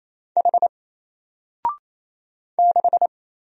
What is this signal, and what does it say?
Morse code, heard as a keyed tone, about 27 wpm, 700 Hz. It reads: H 6
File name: Single-Letter-Number-28wpm